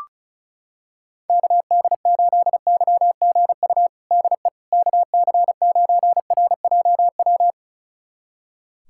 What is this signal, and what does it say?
Morse code, heard as a keyed tone, about 36 wpm, 690 Hz. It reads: KD8YGU DE KC9RJW
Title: USA Callsign DE Callsign - 6 Characters - 35wpm